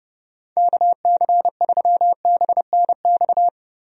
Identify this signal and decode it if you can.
Morse code, heard as a keyed tone, about 30 wpm, 700 Hz. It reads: KC3BNX
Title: USA Call Signs - 6 Characters - Repeated 2x - 30wpm